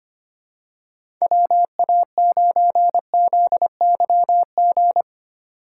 Morse code, heard as a keyed tone, about 25 wpm, 700 Hz. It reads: WA9ZYG
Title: USA Call Signs - 2x Character Spacing - 25wpm